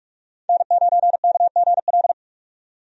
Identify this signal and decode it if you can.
Morse code, heard as a keyed tone, about 44 wpm, 700 Hz. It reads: N9KGL
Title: USA Call Signs - 5 Characters - 45wpm